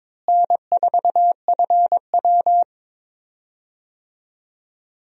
Morse code, heard as a keyed tone, about 22 wpm, 700 Hz. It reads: N4FW
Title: USA Call Signs - 4 Characters - Repeated 3x - 22wpm